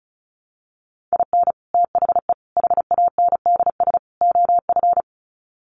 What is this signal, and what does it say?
Morse code, heard as a keyed tone, about 34 wpm, 700 Hz. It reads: IN THE HANDS OF
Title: Sets of 4 Words 35wpm